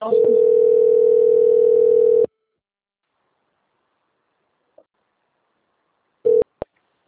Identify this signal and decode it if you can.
Ringback tone (North America), 440 + 480 Hz.